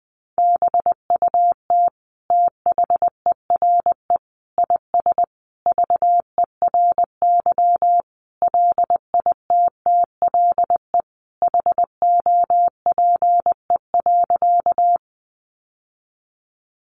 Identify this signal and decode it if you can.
Morse code, heard as a keyed tone, about 19 wpm, 700 Hz. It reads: BUT THERE IS VERY LITTLE HOPE.
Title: Sentences from Top 1000 Words 20wpm